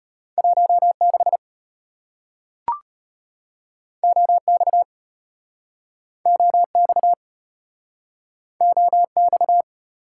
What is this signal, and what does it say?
Morse code, heard as a keyed tone, about 35 wpm, 700 Hz. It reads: 16 OX OX OX